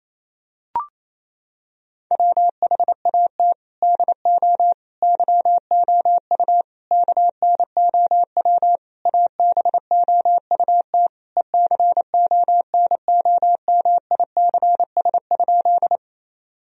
Morse code, heard as a keyed tone, about 28 wpm, 700 Hz. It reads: WHAT DO YOU KNOW ABOUT ECONOMICS?
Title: Sentences from Top 1000 Words - Review Part 3 - 28wpm